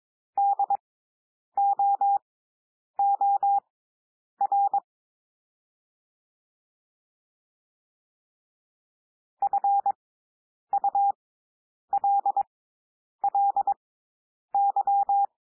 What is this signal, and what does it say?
Morse code, heard as a keyed tone, about 22 wpm, spaced as for 9 wpm, 800 Hz. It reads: DOOR FULLY
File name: SOUND WORDS 22 wpm